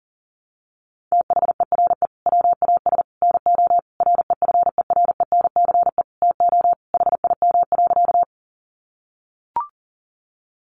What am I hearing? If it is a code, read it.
Morse code, heard as a keyed tone, about 40 wpm, 700 Hz. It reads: THERE WAS NO REFERENCE TO HIM.
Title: Sentences from Top 1200 Words 40wpm